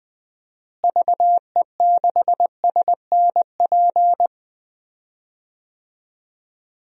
Morse code, heard as a keyed tone, about 20 wpm, 700 Hz. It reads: VE6SNP
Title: DXCC-100 Call Signs - 6 Characters - Repeated 2x - 20wpm